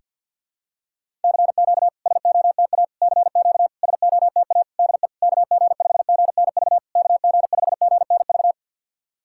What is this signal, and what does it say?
Morse code, heard as a keyed tone, about 50 wpm, 690 Hz. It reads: CQ SOTA CQ SOTA DE KG5GNV KG5GNV
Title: SOTA QSO Practice 50wpm